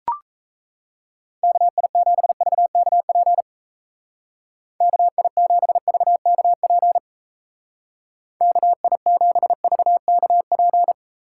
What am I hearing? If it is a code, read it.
Morse code, heard as a keyed tone, about 35 wpm, 700 Hz. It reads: KI7VKP KI7VKP KI7VKP
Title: Speed Racing - USA 2x3 Call Signs 28wpm